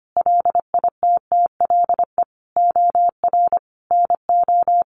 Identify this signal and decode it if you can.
Morse code, heard as a keyed tone, about 25 wpm, 700 Hz. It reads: LITTLE OR NO
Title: Sentences from Top 1400 Words 25wpm